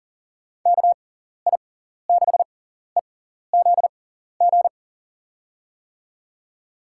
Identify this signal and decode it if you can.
Morse code, heard as a keyed tone, about 39 wpm, spaced as for 14 wpm, 700 Hz. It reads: KI6EZG
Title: USA Call Signs - 6 Characters - 6x Character Spacing - 40wpm